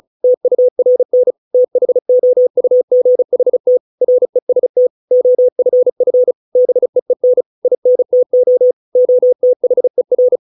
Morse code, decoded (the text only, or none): TURN THOUGHT REST OFF BEEN INTO OTHER